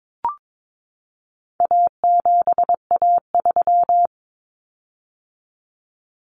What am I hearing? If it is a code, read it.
Morse code, heard as a keyed tone, about 22 wpm, 700 Hz. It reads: A7A3
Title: Common Cut Numbers - 4-Digit 22wpm